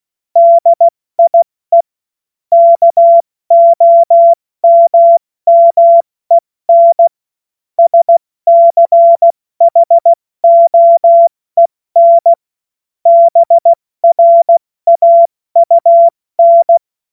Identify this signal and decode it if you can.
Morse code, heard as a keyed tone, about 16 wpm, 680 Hz. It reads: DIE KOMMEN SCHOEN BRAUN
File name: Witz-14-Oktober-2025-70-F80